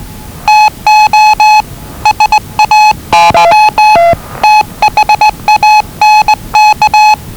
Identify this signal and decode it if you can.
Morse code, heard as a keyed tone, about 20 wpm, 850 Hz. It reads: TO SAY THANK
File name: WellingtonRadioZlwCloseDownOnHfAndMf / 01-Zlw 4277khz 417.5 qrt until 0925z